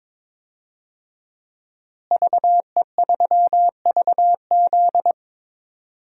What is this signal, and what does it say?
Morse code, heard as a keyed tone, about 22 wpm, 700 Hz. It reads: VE3VZ
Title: DXCC-100 Call Signs - 5 Characters - Repeated 2x - 22wpm